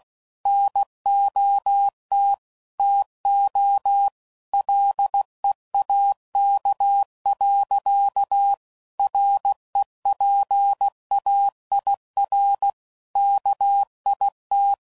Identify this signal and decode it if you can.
Morse code, heard as a keyed tone, about 16 wpm, 800 Hz. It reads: NOT TO LEAK. REPAIR KIT
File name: Sayings / Quotes / Airtight